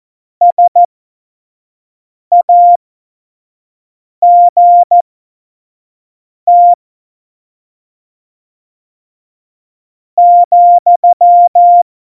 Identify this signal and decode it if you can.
Morse code, heard as a keyed tone, about 14 wpm, spaced as for 4 wpm, 700 Hz. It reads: SAGT,
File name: Morsekurs / Witze / Witz 23. April 2019, 25 F70